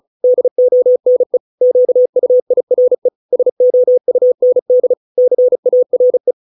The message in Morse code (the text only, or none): DONE QUIRE SOUND CARE